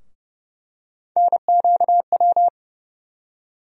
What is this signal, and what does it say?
Morse code, heard as a keyed tone, about 30 wpm, 700 Hz. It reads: NQW
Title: USA Call Sign Postfixes - 3 Letters 30wpm